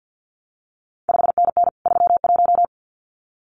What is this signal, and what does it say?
Morse code, heard as a keyed tone, about 50 wpm, 700 Hz. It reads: 5NN 31
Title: CQ WW DX Contest Exchange - 50wpm